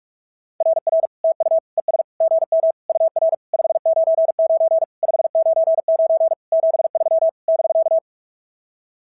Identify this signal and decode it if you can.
Morse code, heard as a keyed tone, about 44 wpm, 690 Hz. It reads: RR TU ES GM UR 599 599 73 <BK>
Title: SOTA QSO Practice 45wpm